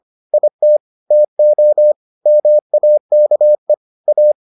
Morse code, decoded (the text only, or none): IT TO MAKE A